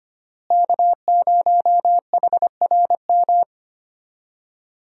Morse code, heard as a keyed tone, about 25 wpm, 700 Hz. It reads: K0HRM
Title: USA Call Signs - 5 Characters - Repeated 2x - 25wpm